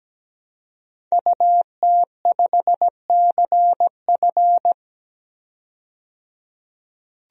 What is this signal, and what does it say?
Morse code, heard as a keyed tone, about 17 wpm, 700 Hz. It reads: UT5CF
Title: DXCC-100 Call Signs - 5 Characters - 17wpm